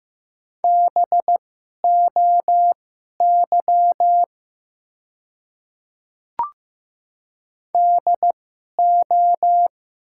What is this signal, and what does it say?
Morse code, heard as a keyed tone, about 15 wpm, spaced as for 11 wpm, 700 Hz. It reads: BOY DO
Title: 2x Character Spacing - Top 200 Words - 15wpm